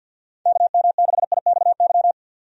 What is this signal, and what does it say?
Morse code, heard as a keyed tone, about 52 wpm, 700 Hz. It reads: KM6IXY
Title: USA Call Signs - 6 Characters - 50wpm